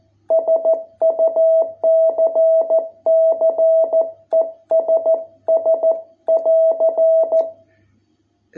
Morse code, heard as a keyed tone, about 14 wpm, 660 Hz. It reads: SUCCESS <AR>